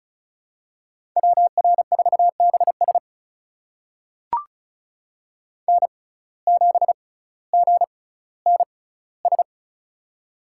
Morse code, heard as a keyed tone, about 36 wpm, spaced as for 18 wpm, 700 Hz. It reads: WR4BS N7GNS
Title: USA Call Signs - 5 Characters - 6x Character Spacing - 35wpm